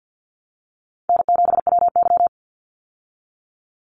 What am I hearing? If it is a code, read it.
Morse code, heard as a keyed tone, about 50 wpm, 700 Hz. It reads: N7WY
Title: Call Signs Repeated 3x - 50wpm